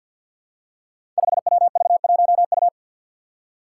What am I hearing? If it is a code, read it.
Morse code, heard as a keyed tone, about 47 wpm, 700 Hz. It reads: 5WV1U
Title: Five Challenging Letter-Number Combinations 50wpm